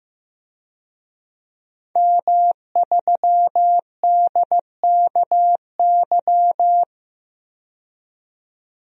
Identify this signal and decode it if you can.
Morse code, heard as a keyed tone, about 15 wpm, 700 Hz. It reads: M3DKY
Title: DXCC-100 Call Signs - 5 Characters - 15wpm